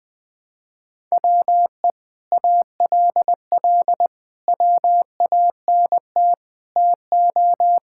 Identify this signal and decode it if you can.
Morse code, heard as a keyed tone, about 21 wpm, 700 Hz. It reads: WE ALL WANT TO
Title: Sentences with 7-Word Phrases - 20wpm